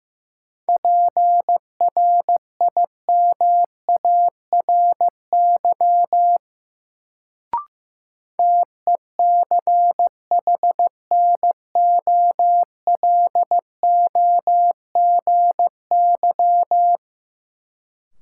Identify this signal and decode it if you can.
Morse code, heard as a keyed tone, about 15 wpm, 700 Hz. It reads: PRIMARY TECHNOLOGY
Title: Sentences from Top 1000 Words - Review Part 5 - New Word Drill - 15wpm